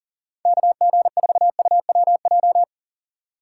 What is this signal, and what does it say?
Morse code, heard as a keyed tone, about 40 wpm, 700 Hz. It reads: KG4UWJ
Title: USA Call Signs - 6 Characters - 6x Character Spacing - 40wpm